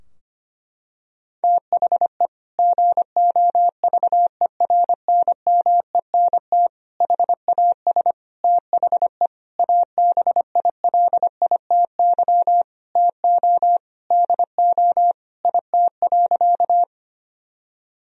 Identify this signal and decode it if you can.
Morse code, heard as a keyed tone, about 25 wpm, 700 Hz. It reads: THE GOVERNMENT HAS THE ABILITY TO DO IT.
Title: Sentences from Top 1000 Words - Review Part 3 - 25wpm